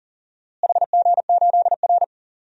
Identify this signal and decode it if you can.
Morse code, heard as a keyed tone, about 39 wpm, 700 Hz. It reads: HG8R
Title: DXCC-100 Call Signs - 4 Characters - Repeated 3x - 40wpm